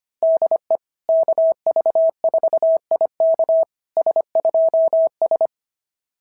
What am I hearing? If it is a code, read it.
Morse code, heard as a keyed tone, about 24 wpm, 650 Hz. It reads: DE KV4IK S2S
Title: SOTA QSO Practice 25wpm